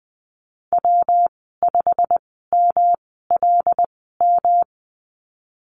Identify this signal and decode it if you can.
Morse code, heard as a keyed tone, about 19 wpm, spaced as for 14 wpm, 700 Hz. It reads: W5MLM
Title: 219 Lesson 35g - Number 0 - Call Signs 20wpm